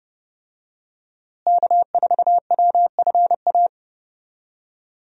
Morse code, heard as a keyed tone, about 32 wpm, 700 Hz. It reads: K4WFA
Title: USA Call Signs - 5 Characters - Repeated 2x - 30wpm